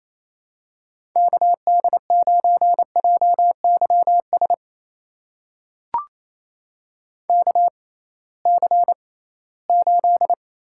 Morse code, heard as a keyed tone, about 28 wpm, spaced as for 18 wpm, 700 Hz. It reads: KD9JYS KC8
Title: USA Call Signs - 6 Characters - 6x Character Spacing - 28wpm